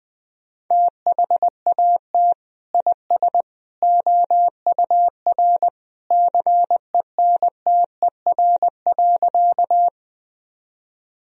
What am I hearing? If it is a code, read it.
Morse code, heard as a keyed tone, about 20 wpm, 700 Hz. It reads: THAT IS OUR CENTER.
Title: Sentences from Top 1000 Words - Review Part 2 - 20wpm